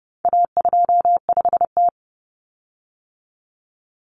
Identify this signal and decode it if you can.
Morse code, heard as a keyed tone, about 30 wpm, 700 Hz. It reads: A25T
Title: Common Cut Numbers - 4-Digit 30wpm